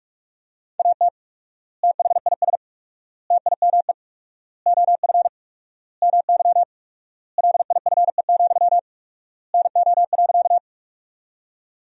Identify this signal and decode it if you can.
Morse code, heard as a keyed tone, about 41 wpm, 700 Hz. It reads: AT THIS TIME OF MY LIFE, NO.
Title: 4x Word-Spacing - Sentences from Top 300 Words - 45wpm